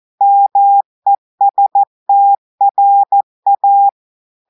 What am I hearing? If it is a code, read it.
Morse code, heard as a keyed tone, about 14 wpm, 800 Hz. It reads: MESTRA
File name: audios / portugais / portugais texte en clair 1 vitesse 14 mots minutes